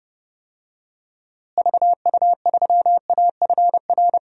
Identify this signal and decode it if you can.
Morse code, heard as a keyed tone, about 30 wpm, 700 Hz. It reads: VU3AFR
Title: DXCC-100 Call Signs - 6 Characters - Repeated 3x - 30wpm